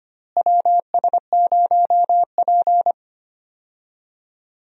Morse code, heard as a keyed tone, about 25 wpm, 700 Hz. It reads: WS0P
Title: USA Call Signs - 4 Characters - Repeated 3x - 25wpm